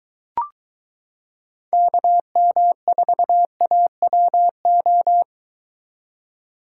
Morse code, heard as a keyed tone, about 23 wpm, 700 Hz. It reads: KM4AWO